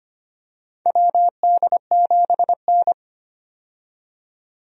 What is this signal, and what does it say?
Morse code, heard as a keyed tone, about 25 wpm, 700 Hz. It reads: WD7N
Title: USA Call Signs - 4 Characters - 25wpm